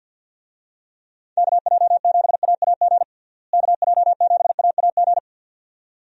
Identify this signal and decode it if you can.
Morse code, heard as a keyed tone, about 51 wpm, 700 Hz. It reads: KJ7AAG KJ7AAG
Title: USA Call Signs - 6 Characters - Repeated 2x - 50wpm